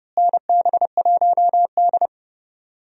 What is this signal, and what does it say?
Morse code, heard as a keyed tone, about 32 wpm, 700 Hz. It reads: NB1D
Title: USA Call Signs - 4 Characters - Repeated 3x - 30wpm